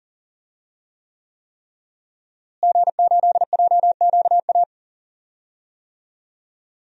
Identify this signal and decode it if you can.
Morse code, heard as a keyed tone, about 40 wpm, 700 Hz. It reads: G8JQA
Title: DXCC-100 Call Signs - 5 Characters - Repeated 3x - 40wpm